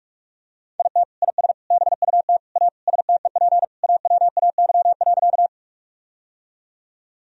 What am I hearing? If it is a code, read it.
Morse code, heard as a keyed tone, about 46 wpm, 700 Hz. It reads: IT IS BUT A STEP AWAY.
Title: Sentences from Top 1000 Words - Review Part 2 - 45wpm